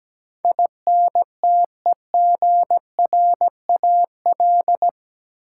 Morse code, heard as a keyed tone, about 17 wpm, 700 Hz. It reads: INTEGRAL
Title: Sets of 5 Words - New Word Drill - 17wpm